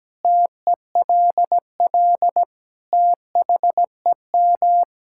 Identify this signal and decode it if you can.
Morse code, heard as a keyed tone, about 17 wpm, 700 Hz. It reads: TELL THEM
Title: Sentences from Top 1000 Words - Review Part 6 - 17wpm